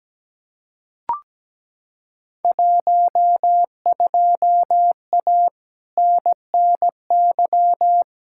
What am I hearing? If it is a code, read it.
Morse code, heard as a keyed tone, about 17 wpm, 700 Hz. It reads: 12A NNY
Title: Field Day Exchanges Encore - 17wpm